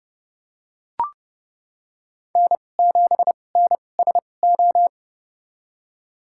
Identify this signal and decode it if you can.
Morse code, heard as a keyed tone, about 32 wpm, spaced as for 22 wpm, 700 Hz. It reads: N7NSO
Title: USA Call Signs - 5 Characters - 2x Character Spacing - 30wpm